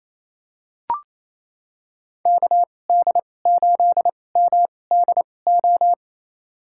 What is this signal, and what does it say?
Morse code, heard as a keyed tone, about 28 wpm, spaced as for 20 wpm, 700 Hz. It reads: KD8MDO
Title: USA Call Signs - 6 Characters - 2x Character Spacing - 28wpm